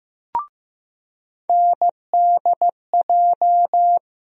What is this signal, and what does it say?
Morse code, heard as a keyed tone, about 15 wpm, 700 Hz. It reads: NDJ
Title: USA Call Sign Postfixes - 3 Letters 15wpm